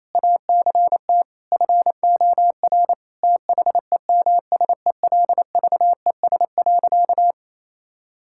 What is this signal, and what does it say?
Morse code, heard as a keyed tone, about 28 wpm, 700 Hz. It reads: ACT FOR THEMSELVES.
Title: Sentences from Top 1000 Words - Review Part 5 - 28wpm